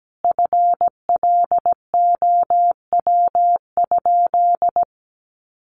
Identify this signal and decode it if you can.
Morse code, heard as a keyed tone, about 17 wpm, 700 Hz. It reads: FLOW?
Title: Sentences from Top 1200 Words 17wpm